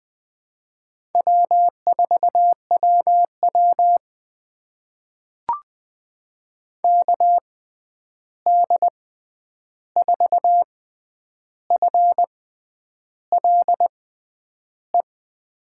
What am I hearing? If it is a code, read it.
Morse code, heard as a keyed tone, about 21 wpm, spaced as for 9 wpm, 700 Hz. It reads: W4WW KD4FLE
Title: USA Call Signs - 6x Character Spacing - 20wpm